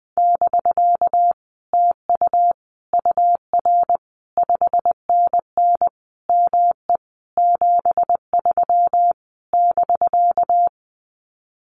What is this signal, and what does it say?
Morse code, heard as a keyed tone, about 20 wpm, 700 Hz. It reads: <BK> TU UR 5NN ME 73 <BK>
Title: Common QSO Phrases Encore 20wpm